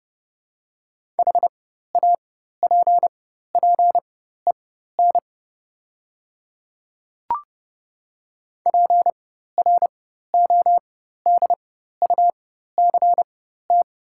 Morse code, heard as a keyed tone, about 30 wpm, spaced as for 14 wpm, 690 Hz. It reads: HAPPEN PRODUCT
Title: 4x Character Spacing - Top 500 Words - 30wpm